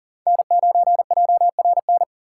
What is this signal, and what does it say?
Morse code, heard as a keyed tone, about 38 wpm, 700 Hz. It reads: N9JRN
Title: USA Call Signs - 5 Characters - 6x Character Spacing - 40wpm